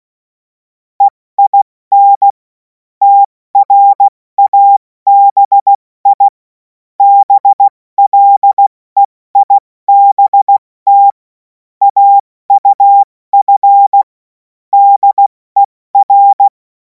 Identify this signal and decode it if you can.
Morse code, heard as a keyed tone, about 16 wpm, spaced as for 14 wpm, 800 Hz. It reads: EIN TRABI BLEIBT AUF DER
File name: Morsekurs / Witze / Witz 05. Mai 2026 70 F80